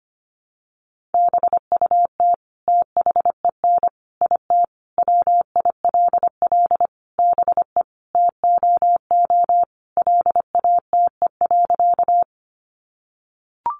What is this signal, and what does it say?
Morse code, heard as a keyed tone, about 25 wpm, 700 Hz. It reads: BUT THEN IT WILL BE TOO LATE.
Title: Sentences from Top 300 Words 25wpm